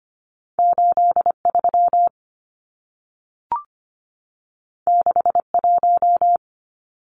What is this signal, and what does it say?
Morse code, heard as a keyed tone, about 25 wpm, 700 Hz. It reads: 83 61
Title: Numbers - 2-Digit 25wpm